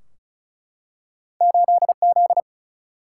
Morse code, heard as a keyed tone, about 36 wpm, 700 Hz. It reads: 8Z
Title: Single-Letter Single-Number 35wpm